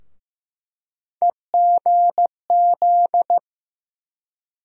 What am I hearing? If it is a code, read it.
Morse code, heard as a keyed tone, about 15 wpm, 700 Hz. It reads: EGZ
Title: USA Call Sign Postfixes - 3 Letters 15wpm